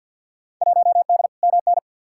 Morse code, heard as a keyed tone, about 53 wpm, 700 Hz. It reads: 1D MN
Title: Field Day Exchanges - 50wpm